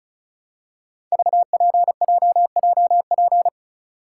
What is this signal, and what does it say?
Morse code, heard as a keyed tone, about 36 wpm, 700 Hz. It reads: VPJJP
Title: Code Groups - 5 Letters - 35wpm